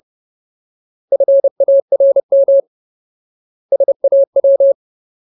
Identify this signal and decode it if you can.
Morse code, heard as a keyed tone, about 30 wpm, 550 Hz. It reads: FARM SAW
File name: Ultimate Word List / 30WPM Ultimate